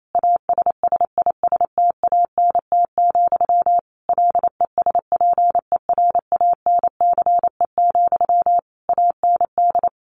Morse code, heard as a keyed tone, about 28 wpm, 700 Hz. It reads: ASSISTANT, LESPERANCE, AND
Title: A Sound Of Thunder 28wpm